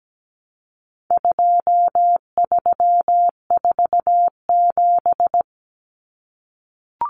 Morse code, heard as a keyed tone, about 17 wpm, 700 Hz. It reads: 2347
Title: Numbers - 4-Digit 17wpm